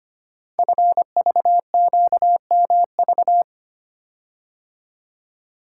Morse code, heard as a keyed tone, about 25 wpm, 700 Hz. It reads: FVQMV
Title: Code Groups - 5 Letter:Numbers - 25wpm